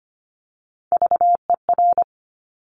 Morse code, heard as a keyed tone, about 25 wpm, 700 Hz. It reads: VER
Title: Top Three Letter Combinations in Words 25wpm